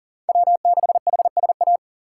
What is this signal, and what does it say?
Morse code, heard as a keyed tone, about 40 wpm, 690 Hz. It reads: W6HSA
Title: USA Call Signs - 5 Characters - 3x Character Spacing - 40wpm